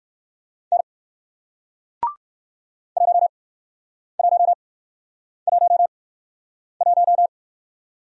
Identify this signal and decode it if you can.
Morse code, heard as a keyed tone, about 57 wpm, 700 Hz. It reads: N 1 1 1 1